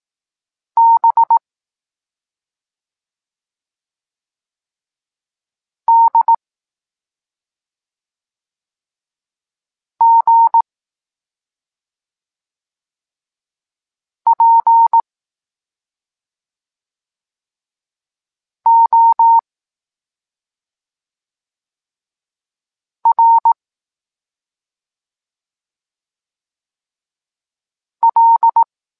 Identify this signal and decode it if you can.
Morse code, heard as a keyed tone, about 18 wpm, 920 Hz. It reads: B D G P O R L
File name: Tekst-26 3